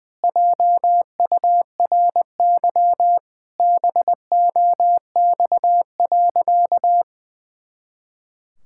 Morse code, heard as a keyed tone, about 20 wpm, 700 Hz. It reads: JURY BOX.
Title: Pangram Challenge v2 - 20wpm